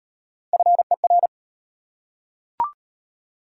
Morse code, heard as a keyed tone, about 34 wpm, 700 Hz. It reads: FER